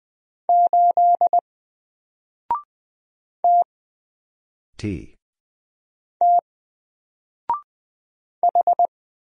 Morse code, heard as a keyed tone, about 20 wpm, 700 Hz. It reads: 8 T T H
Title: Single Character (Letter:Number:Punctuation:Prosign) Alt 20wpm